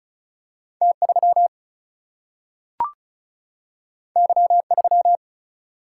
Morse code, heard as a keyed tone, about 36 wpm, 700 Hz. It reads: T 3 Y 3
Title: Single-Letter Single-Number 35wpm